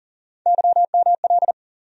Morse code, heard as a keyed tone, about 40 wpm, 700 Hz. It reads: YML
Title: USA Call Sign Postfixes - 3 Letters 40wpm